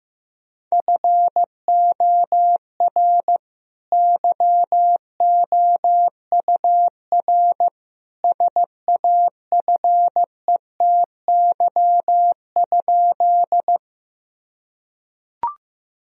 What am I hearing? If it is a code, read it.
Morse code, heard as a keyed tone, about 15 wpm, 700 Hz. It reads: FOR YOUR SAFETY?
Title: Sentences from Top 1000 Words - Review Part 4 - 15wpm